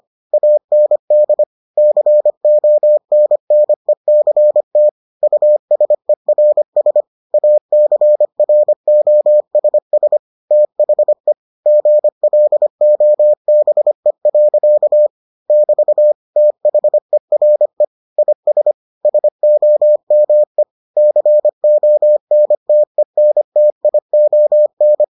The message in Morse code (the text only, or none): AND CONNECT USERS ACROSS THE GLOBE. <BT> THERE IS SOME CONTENTION